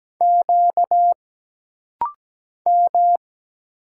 Morse code, heard as a keyed tone, about 17 wpm, 700 Hz. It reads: Q M
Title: Single Letters - Rapid-Fire 17wpm